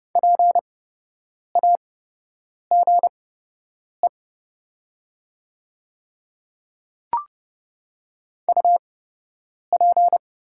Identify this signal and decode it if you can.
Morse code, heard as a keyed tone, about 30 wpm, spaced as for 8 wpm, 700 Hz. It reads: PAGE UP
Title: 8x Character Spacing - Top 300 Words - 30wpm